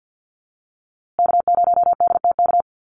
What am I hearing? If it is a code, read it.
Morse code, heard as a keyed tone, about 50 wpm, 700 Hz. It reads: K0DTK
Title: USA Call Signs Repeated 3x 50wpm